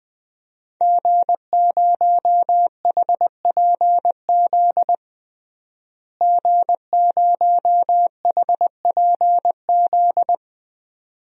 Morse code, heard as a keyed tone, about 21 wpm, 700 Hz. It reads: G0HPZ G0HPZ
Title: DXCC-100 Call Signs - 5 Characters - Repeated 2x - 20wpm